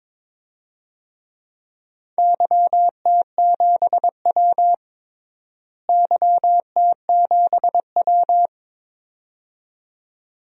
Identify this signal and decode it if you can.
Morse code, heard as a keyed tone, about 22 wpm, 700 Hz. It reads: YT7W YT7W
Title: DXCC-100 Call Signs - 4 Characters - Repeated 2x - 22wpm